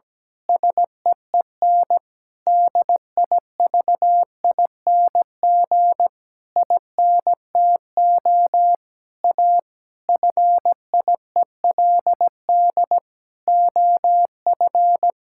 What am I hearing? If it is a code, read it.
Morse code, heard as a keyed tone, about 17 wpm, 700 Hz. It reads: SEEN DIVING INTO A FIELD OF
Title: Pangram Challenge v2 - 17wpm